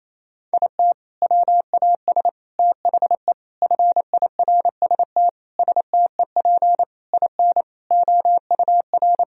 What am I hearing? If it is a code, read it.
Morse code, heard as a keyed tone, about 28 wpm, 700 Hz. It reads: IT WAS THE FIRST STEP IN OUR
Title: Sentences from Top 1000 Words - Review Part 4 - 28wpm